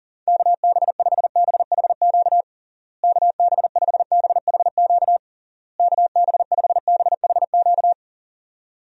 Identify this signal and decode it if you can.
Morse code, heard as a keyed tone, about 40 wpm, 700 Hz. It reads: KB5BHQ KB5BHQ KB5BHQ
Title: USA Call Signs - 6 Characters - Repeated 3x - 40wpm